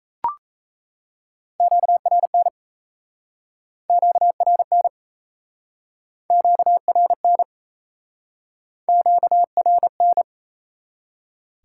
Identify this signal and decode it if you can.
Morse code, heard as a keyed tone, about 37 wpm, 700 Hz. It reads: QRN QRN QRN QRN